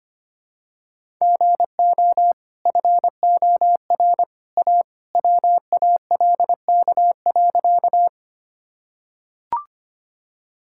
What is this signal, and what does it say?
Morse code, heard as a keyed tone, about 25 wpm, 700 Hz. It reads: GO FOR A WALK.
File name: Easy Sentences with Hints - 25wpm